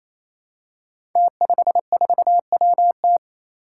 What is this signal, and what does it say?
Morse code, heard as a keyed tone, about 28 wpm, 700 Hz. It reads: T54WT
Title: Five Challenging Letter-Number Combinations 28wpm